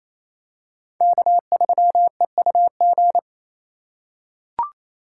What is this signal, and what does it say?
Morse code, heard as a keyed tone, about 27 wpm, 700 Hz. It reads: K3EUG
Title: USA Call Signs - 5 Characters - 4x Character Spacing - 28wpm